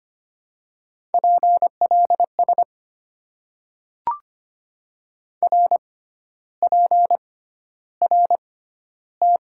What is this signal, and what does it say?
Morse code, heard as a keyed tone, about 25 wpm, spaced as for 12 wpm, 700 Hz. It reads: PLS RPRT
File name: Top 90 QSO Elements - 6x Character Spacing - 25wpm